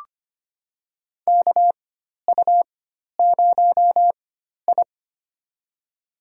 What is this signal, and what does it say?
Morse code, heard as a keyed tone, about 25 wpm, spaced as for 12 wpm, 700 Hz. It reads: KU0I
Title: USA Call Signs - 4 Characters - 4x Character Spacing - 25wpm